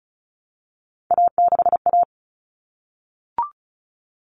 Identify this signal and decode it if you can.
Morse code, heard as a keyed tone, about 34 wpm, 700 Hz. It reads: A6A
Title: Common Cut Numbers - 3-Digit 35wpm